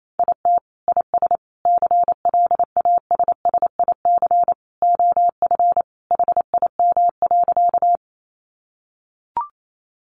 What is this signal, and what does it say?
Morse code, heard as a keyed tone, about 28 wpm, 700 Hz. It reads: IT IS CLASSIC OF HIM.
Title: Sentences from Top 1000 Words 28wpm